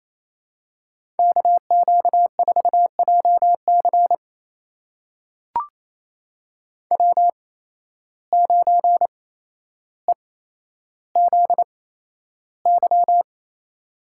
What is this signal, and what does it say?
Morse code, heard as a keyed tone, about 28 wpm, spaced as for 12 wpm, 700 Hz. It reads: KQ4JC W9EZY
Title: USA Call Signs - 5 Characters - 8x Character Spacing - 28wpm